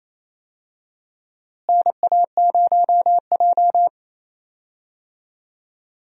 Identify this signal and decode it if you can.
Morse code, heard as a keyed tone, about 28 wpm, 700 Hz. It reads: NA0J
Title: USA Call Signs - 4 Characters - Repeated 2x - 28wpm